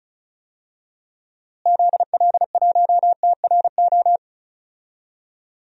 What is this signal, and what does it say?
Morse code, heard as a keyed tone, about 36 wpm, 700 Hz. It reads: ZL1TRO
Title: DXCC-100 Call Signs - 6 Characters - Repeated 2x - 35wpm